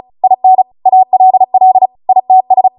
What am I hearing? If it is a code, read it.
Morse code, heard as a keyed tone, about 36 wpm, 750 Hz. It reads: IN ALL ITS